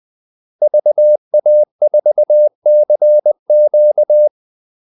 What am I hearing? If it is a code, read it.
Morse code, heard as a keyed tone, about 20 wpm, 600 Hz. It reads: VA4CQ
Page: Kaldesignaler 20 wpm